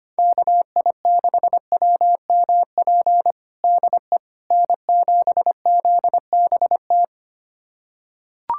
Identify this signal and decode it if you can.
Morse code, heard as a keyed tone, about 25 wpm, 700 Hz. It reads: KI6WMP DE N7ZBT
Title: USA Callsign DE Callsign 25wpm